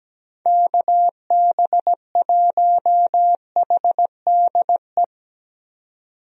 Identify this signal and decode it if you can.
Morse code, heard as a keyed tone, about 17 wpm, 700 Hz. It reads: KB1HDE
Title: USA Call Signs - 6 Characters - 17wpm